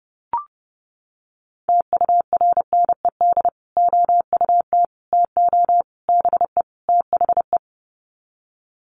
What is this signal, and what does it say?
Morse code, heard as a keyed tone, about 30 wpm, 700 Hz. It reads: TURNED OUT TO BE THE
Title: Sets of 5 Words Encore 30wpm